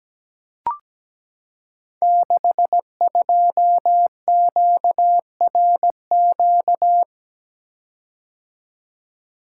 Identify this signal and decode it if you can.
Morse code, heard as a keyed tone, about 17 wpm, 700 Hz. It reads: 62QRQ
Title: Code Groups - 5 Letter:Numbers - 17wpm